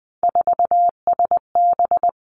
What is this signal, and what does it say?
Morse code, heard as a keyed tone, about 20 wpm, 700 Hz. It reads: 4SB
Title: Three Challenging Letter-Number Combinations 20wpm